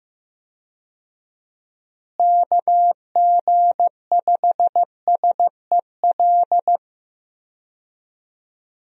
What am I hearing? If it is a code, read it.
Morse code, heard as a keyed tone, about 16 wpm, 700 Hz. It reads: KG5SEL
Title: USA Call Signs - 6 Characters - Repeated 3x - 15wpm